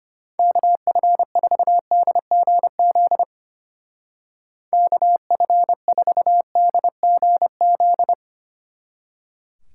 Morse code, heard as a keyed tone, about 31 wpm, 700 Hz. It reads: KF4DGZ KF4DGZ
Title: Speed Racing - USA 2x3 Call Signs 25wpm